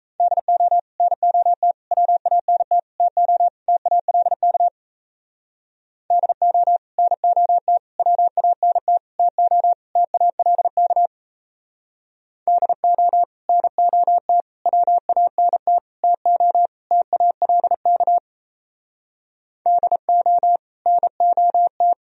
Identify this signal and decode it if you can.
Morse code, heard as a keyed tone, about 35 wpm, 700 Hz. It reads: DO NOT WANT TO TALK DO NOT WANT TO TALK DO NOT WANT TO TALK DO NOT